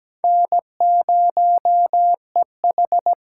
Morse code, heard as a keyed tone, about 20 wpm, 700 Hz. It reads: N0EH
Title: Speed Racing - USA 1x2 Call Signs 17wpm